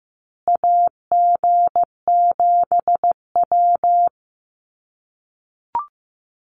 Morse code, heard as a keyed tone, about 15 wpm, 700 Hz. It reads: AG7W
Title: USA Call Signs Repeated 3x 15wpm